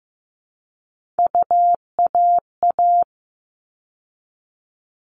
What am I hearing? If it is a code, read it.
Morse code, heard as a keyed tone, about 15 wpm, 700 Hz. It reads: UAA
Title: All Cut Numbers - 3-Digit 15wpm